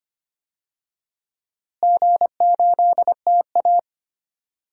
Morse code, heard as a keyed tone, about 25 wpm, 700 Hz. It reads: G8TA
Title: DXCC-100 Call Signs - 4 Characters - Repeated 3x - 25wpm